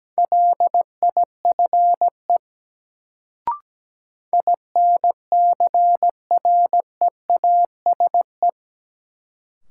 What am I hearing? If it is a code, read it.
Morse code, heard as a keyed tone, about 17 wpm, 700 Hz. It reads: LIFE INCREASE
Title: Sets of 5 Words - New Word Drill - 17wpm